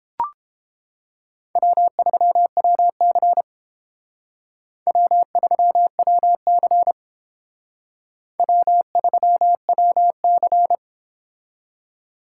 Morse code, heard as a keyed tone, about 29 wpm, 700 Hz. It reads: W3WC W3WC W3WC